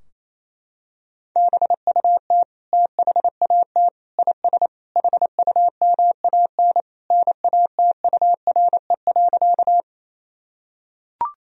Morse code, heard as a keyed tone, about 27 wpm, 700 Hz. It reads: BUT THAT IS HUMAN NATURE.
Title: Sentences from Top 1000 Words - Review Part 3 - 28wpm